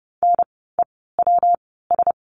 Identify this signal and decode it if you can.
Morse code, heard as a keyed tone, about 30 wpm, 700 Hz. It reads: N E W S
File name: course / 30:10wpm / 115 Lesson 19f - Letter w - Words 30wpm